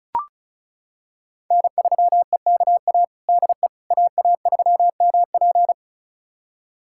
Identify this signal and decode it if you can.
Morse code, heard as a keyed tone, about 36 wpm, 700 Hz. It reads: N3EKA DE AA3MP
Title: USA Callsign DE Callsign - 5 Characters - 35wpm